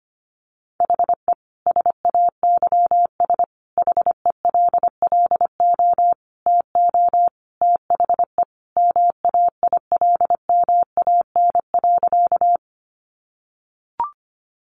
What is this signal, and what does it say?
Morse code, heard as a keyed tone, about 25 wpm, 700 Hz. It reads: HE SAYS HELLO TO THE MAILMAN.
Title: Additional Easy Phrases 25wpm